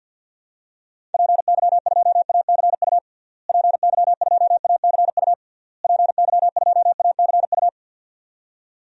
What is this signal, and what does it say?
Morse code, heard as a keyed tone, about 52 wpm, 700 Hz. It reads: PY2ACU PY2ACU PY2ACU
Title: DXCC-100 Call Signs - 6 Characters - Repeated 3x - 50wpm